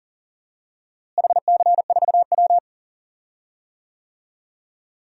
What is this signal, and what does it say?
Morse code, heard as a keyed tone, about 40 wpm, 700 Hz. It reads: HC4W
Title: DXCC-100 Call Signs - 4 Characters - Repeated 3x - 40wpm